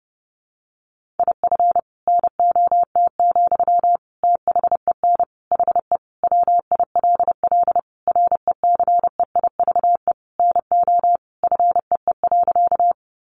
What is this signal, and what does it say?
Morse code, heard as a keyed tone, about 30 wpm, 700 Hz. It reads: IF NOT, THEN HE WILL RECEIVE NO FEE.
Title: Sentences from Top 1500 Words 30wpm